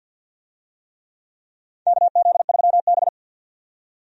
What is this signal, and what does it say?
Morse code, heard as a keyed tone, about 52 wpm, 700 Hz. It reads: KZ3B
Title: USA Call Signs - 4 Characters - 50wpm